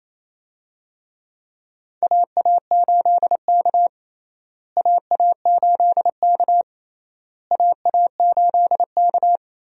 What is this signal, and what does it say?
Morse code, heard as a keyed tone, about 28 wpm, 700 Hz. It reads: AA8K AA8K AA8K
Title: USA Call Signs - 4 Characters - Repeated 3x - 28wpm